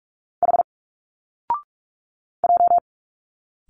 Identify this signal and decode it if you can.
Morse code, heard as a keyed tone, about 43 wpm, 700 Hz. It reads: H J
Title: Single Letters 45wpm